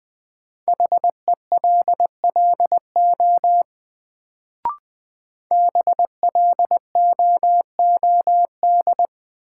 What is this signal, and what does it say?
Morse code, heard as a keyed tone, about 20 wpm, 700 Hz. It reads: HELLO BLOOD
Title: Sentences from Top 1100 Words - New Word Drill Alt - 20wpm